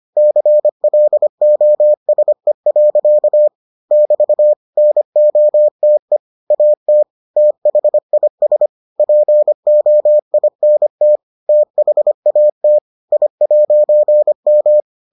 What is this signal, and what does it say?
Morse code, heard as a keyed tone, about 25 wpm, 600 Hz. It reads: CLOSE. <BT> NOTE AT THIS POINT THAT I'M